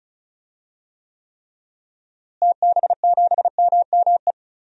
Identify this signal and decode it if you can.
Morse code, heard as a keyed tone, about 34 wpm, 700 Hz. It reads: TB7MME
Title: DXCC-100 Call Signs - 6 Characters - Repeated 3x - 35wpm